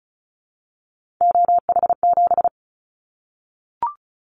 Morse code, heard as a keyed tone, about 36 wpm, 700 Hz. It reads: OH7
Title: Callsign Prefixes for Top 50 DXCC - 35wpm